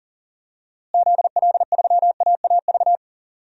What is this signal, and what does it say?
Morse code, heard as a keyed tone, about 40 wpm, 700 Hz. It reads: ZL3AAV
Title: DXCC-100 Call Signs - 6 Characters - Repeated 2x - 40wpm